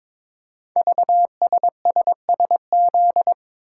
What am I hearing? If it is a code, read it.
Morse code, heard as a keyed tone, about 22 wpm, 700 Hz. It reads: VSSSZ
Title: Code Groups - 5 Letters - 22wpm